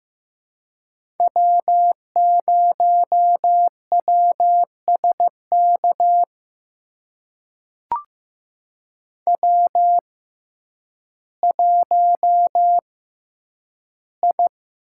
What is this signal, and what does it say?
Morse code, heard as a keyed tone, about 15 wpm, spaced as for 9 wpm, 700 Hz. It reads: W0WSK W1I
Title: USA Call Signs - 5 Characters - 6x Character Spacing - 15wpm